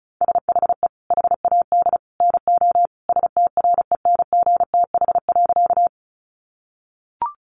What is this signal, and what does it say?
Morse code, heard as a keyed tone, about 34 wpm, 700 Hz. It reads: SHE HAD NO STRENGTH.
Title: Sentences from Top 1300 Words 35wpm